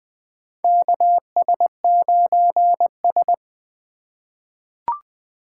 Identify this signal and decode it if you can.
Morse code, heard as a keyed tone, about 20 wpm, 700 Hz. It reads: KS9S
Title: USA Call Signs - 4 Characters - 8x Character Spacing - 20wpm